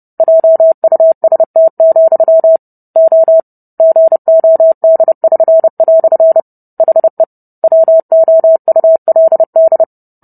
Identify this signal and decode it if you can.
Morse code, heard as a keyed tone, about 30 wpm, 650 Hz. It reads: JUST, O GOD<SN>" HE WOULD